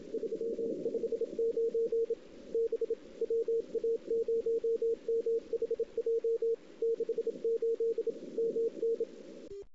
Morse code, heard as a keyed tone, about 26 wpm, 460 Hz, with radio noise. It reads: 359 B WA0MHJ 68 MN
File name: CW / 2008-ss-cw-k5tr-op-n5rz -19:00